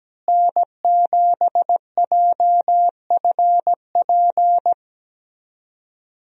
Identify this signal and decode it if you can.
Morse code, heard as a keyed tone, about 17 wpm, 700 Hz. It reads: N7JFP
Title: USA Call Signs - 5 Characters - Repeated 3x - 17wpm